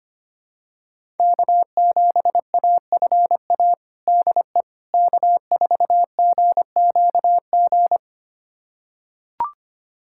Morse code, heard as a keyed tone, about 25 wpm, 700 Hz. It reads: K7AFA DE K4GQG
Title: USA Callsign DE Callsign - 5 Characters - 25wpm